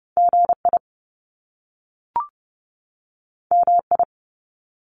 Morse code, heard as a keyed tone, about 30 wpm, 700 Hz. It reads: GI MI
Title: Top Two Letter Combinations in Words 30wpm